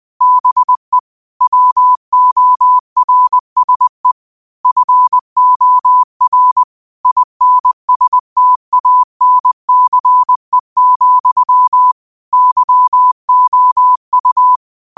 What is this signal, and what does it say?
Morse code, heard as a keyed tone, about 20 wpm, 1000 Hz. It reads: BE WORSE FOR INSTANCE, YOU